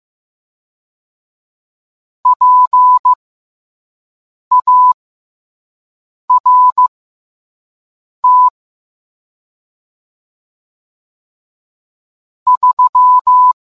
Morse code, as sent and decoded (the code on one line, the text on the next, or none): .--. .- .-. - / ...--
PART 3